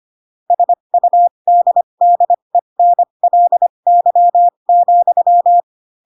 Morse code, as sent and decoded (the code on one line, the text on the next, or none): ... ..- -.. -.. -. .-.. -.-- --..--
SUDDENLY,